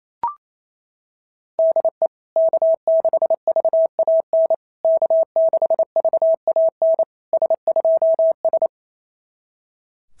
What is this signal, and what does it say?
Morse code, heard as a keyed tone, about 27 wpm, 660 Hz. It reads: DE K6VAN K6VAN S2S
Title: SOTA QSO Practice 28wpm